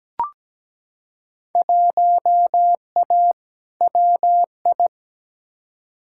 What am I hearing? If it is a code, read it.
Morse code, heard as a keyed tone, about 17 wpm, 700 Hz. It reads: 1A WI
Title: Field Day Exchanges - 17wpm